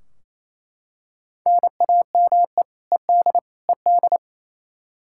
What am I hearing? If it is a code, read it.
Morse code, heard as a keyed tone, about 28 wpm, 700 Hz. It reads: NAME ED ED
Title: Practice QSO - Exchange 1 - Beginning - 28wpm